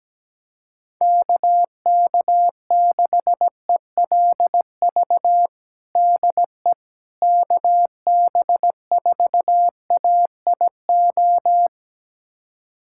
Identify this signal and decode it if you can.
Morse code, heard as a keyed tone, about 17 wpm, 700 Hz. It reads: KK6ELV DE KB4AIO
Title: USA Callsign DE Callsign - 6 Characters - 17wpm